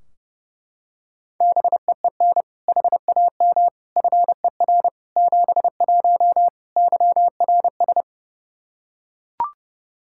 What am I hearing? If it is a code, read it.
Morse code, heard as a keyed tone, about 30 wpm, 700 Hz. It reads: BEEN HAM FER 71 YRS
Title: Practice QSO - Exchange 3 - More Info - 30wpm